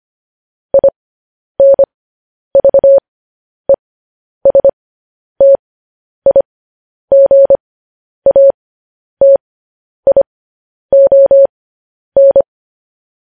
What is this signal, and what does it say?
Morse code, heard as a keyed tone, about 25 wpm, spaced as for 10 wpm, 550 Hz. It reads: INVESTIGATION
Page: news-headlines-in-morse-code-at-10-wpm